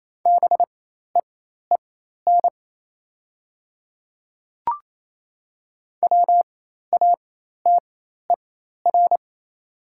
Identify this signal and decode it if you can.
Morse code, heard as a keyed tone, about 28 wpm, spaced as for 13 wpm, 690 Hz. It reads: BEEN WATER
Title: 4x Character Spacing - Top 500 Words - 28wpm